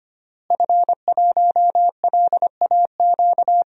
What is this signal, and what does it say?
Morse code, heard as a keyed tone, about 25 wpm, 700 Hz. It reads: F1LAQ
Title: DXCC-100 Call Signs - 5 Characters - Repeated 2x - 25wpm